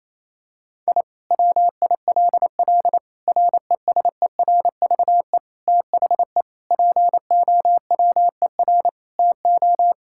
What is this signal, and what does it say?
Morse code, heard as a keyed tone, about 27 wpm, 700 Hz. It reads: I WILL RESERVE THE POWER TO
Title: Sentences from Top 1000 Words - Review Part 6 - 28wpm